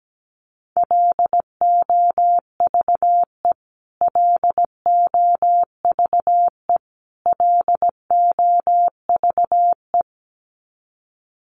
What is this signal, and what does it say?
Morse code, heard as a keyed tone, about 17 wpm, 700 Hz. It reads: LOVE LOVE LOVE
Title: Top 500 Words Repeated 17wpm